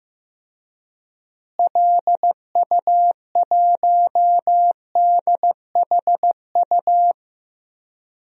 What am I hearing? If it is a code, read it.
Morse code, heard as a keyed tone, about 15 wpm, 700 Hz. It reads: LU1DHU
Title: DXCC-100 Call Signs - 6 Characters - Repeated 2x - 15wpm